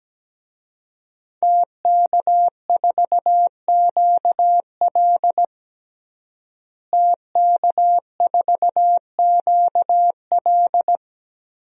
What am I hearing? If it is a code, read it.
Morse code, heard as a keyed tone, about 17 wpm, 700 Hz. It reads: TK4QL TK4QL
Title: DXCC-100 Call Signs - 5 Characters - Repeated 3x - 17wpm